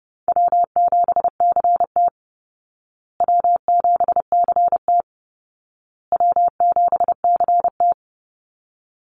Morse code, heard as a keyed tone, about 30 wpm, 700 Hz. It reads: W7CT W7CT W7CT
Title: Easy Call Signs Repeated 3x - 30wpm